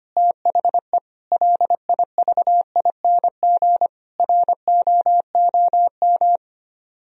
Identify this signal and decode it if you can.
Morse code, heard as a keyed tone, about 31 wpm, 700 Hz. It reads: THE LIVING ROOM
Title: Speed-Racing - Sets of 4 Words 25wpm